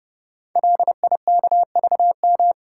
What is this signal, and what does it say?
Morse code, heard as a keyed tone, about 30 wpm, 700 Hz. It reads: LIKVM
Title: Code Groups - 5 Letters - 30wpm